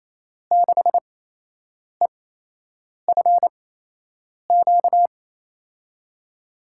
Morse code, heard as a keyed tone, about 28 wpm, spaced as for 8 wpm, 700 Hz. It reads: 6EFQ
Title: USA Call Signs - 6 Characters - 8x Character Spacing - 28wpm